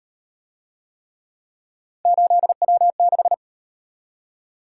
Morse code, heard as a keyed tone, about 36 wpm, 700 Hz. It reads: 8W6